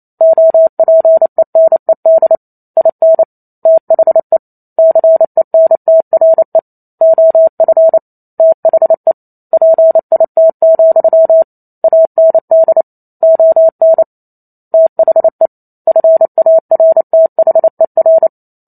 Morse code, heard as a keyed tone, about 28 wpm, 650 Hz. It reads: OPENED IN THE CENTRE OF THE PIT, AND ON THE FARTHER